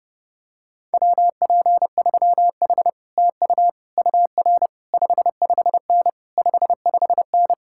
Morse code, heard as a keyed tone, about 32 wpm, 700 Hz. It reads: WP3H TU UR 55N 55N